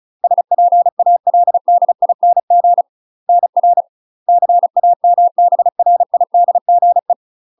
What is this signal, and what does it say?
Morse code, heard as a keyed tone, about 36 wpm, 700 Hz. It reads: SPALDING NR CAMBRIDGE
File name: UK MASTER QSOs TRAINING / 35WPM / UK MASTER QSOs 0035